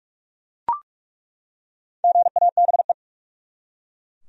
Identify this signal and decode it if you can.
Morse code, heard as a keyed tone, about 43 wpm, 700 Hz. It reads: GABE
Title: USA Common Names - 45wpm